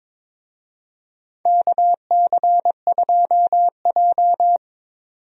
Morse code, heard as a keyed tone, about 22 wpm, 700 Hz. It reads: KC2J
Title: USA Call Signs - 4 Characters - 22wpm